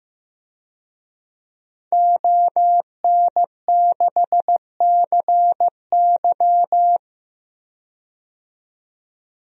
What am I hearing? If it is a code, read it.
Morse code, heard as a keyed tone, about 15 wpm, 700 Hz. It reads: ON6CY
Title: DXCC-100 Call Signs - 5 Characters - 15wpm